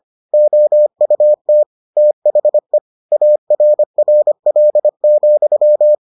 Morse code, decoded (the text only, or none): OUT THE ARRL,